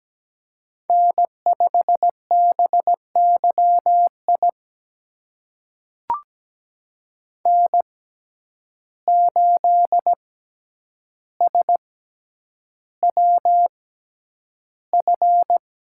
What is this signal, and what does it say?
Morse code, heard as a keyed tone, about 17 wpm, spaced as for 9 wpm, 700 Hz. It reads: N5BYI N8SWF
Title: USA Call Signs - 5 Characters - 6x Character Spacing - 17wpm